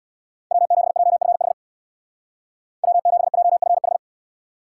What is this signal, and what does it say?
Morse code, heard as a keyed tone, about 61 wpm, 700 Hz. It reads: W7JFH W7JFH
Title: Speed Racing - USA 1x3 Call Signs 50wpm